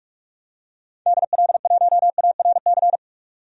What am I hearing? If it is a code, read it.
Morse code, heard as a keyed tone, about 47 wpm, 700 Hz. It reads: DL1ARC
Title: DXCC-100 Call Signs - 6 Characters - 45wpm